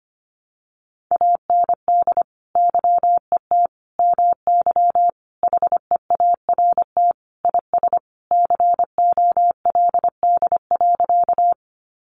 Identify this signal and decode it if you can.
Morse code, heard as a keyed tone, about 25 wpm, 700 Hz. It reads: AND YET MY HEART IS COLD.
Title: Sentences from Top 1200 Words 25wpm